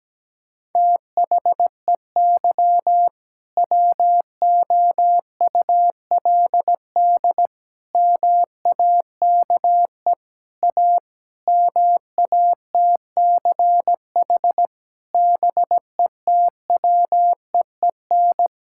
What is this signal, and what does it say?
Morse code, heard as a keyed tone, about 17 wpm, 700 Hz. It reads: THEY WOULD MAKE A MATCH BETWEEN
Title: Sentences from Top 1600 Words 17wpm